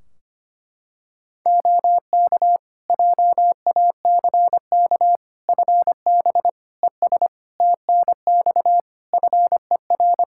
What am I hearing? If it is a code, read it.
Morse code, heard as a keyed tone, about 25 wpm, 700 Hz. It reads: OK JACK FB ES TNX FER
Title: Practice QSO - Exchange 2 - Info - 25wpm